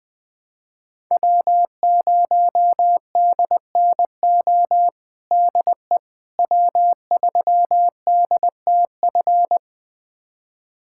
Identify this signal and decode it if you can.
Morse code, heard as a keyed tone, about 20 wpm, 700 Hz. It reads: W0DNO DE W3DTF
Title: USA Callsign DE Callsign - 5 Characters - 20wpm